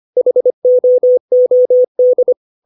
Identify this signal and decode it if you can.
Morse code, heard as a keyed tone, about 25 wpm, 500 Hz. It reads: HOOD
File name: 632 words / 632 Words 25WPM